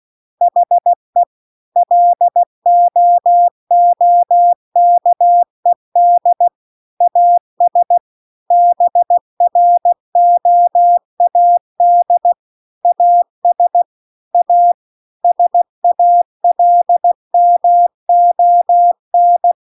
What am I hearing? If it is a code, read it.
Morse code, heard as a keyed tone, about 16 wpm, 700 Hz. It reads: HE LOOKED AS BROAD AS A SALMON
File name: CW-The-Big-Twoheaded-River-2-16WPM